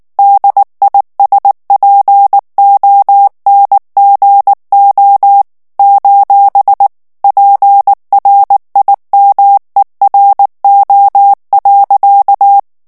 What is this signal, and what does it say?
Morse code, heard as a keyed tone, about 19 wpm, 800 Hz. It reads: DISPONGO: PRIMERO.
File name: morse mp3 / ESPAGNOL TEXTE EN CLAIR 2 VITESSE 18 MOTS MINUTES